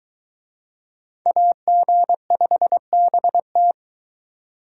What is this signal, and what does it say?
Morse code, heard as a keyed tone, about 22 wpm, 700 Hz. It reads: AG5BT